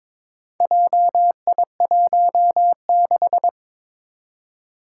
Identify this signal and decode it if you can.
Morse code, heard as a keyed tone, about 22 wpm, 700 Hz. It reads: JI16
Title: Four Challenging Letter-Number Combinations 22wpm